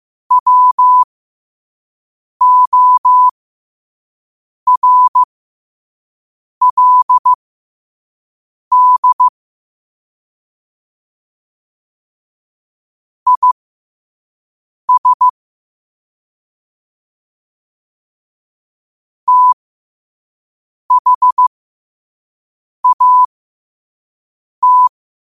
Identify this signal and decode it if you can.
Morse code, heard as a keyed tone, about 15 wpm, spaced as for 5 wpm, 1000 Hz. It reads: WORLD IS THAT
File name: 5wpm